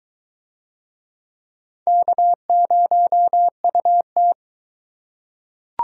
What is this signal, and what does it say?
Morse code, heard as a keyed tone, about 22 wpm, 700 Hz. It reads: K0UT